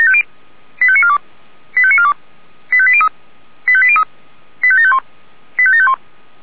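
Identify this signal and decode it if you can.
Five-tone selective call (ZVEI), heard as quick runs of five tones, tones of about 70 ms. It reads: 85942 85942 85803 85803 85751 85751
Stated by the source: Five-tone ZVEI selective call 85860 received on an SDR.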